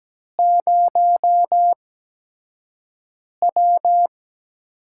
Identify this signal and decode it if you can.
Morse code, heard as a keyed tone, about 17 wpm, spaced as for 5 wpm, 700 Hz. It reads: 0W
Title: USA Call Signs - 4 Characters - 8x Character Spacing - 17wpm